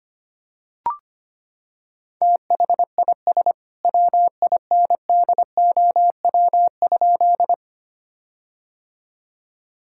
Morse code, heard as a keyed tone, about 25 wpm, 700 Hz. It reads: THIS WINDOW?
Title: Sentences from Top 1600 Words 25wpm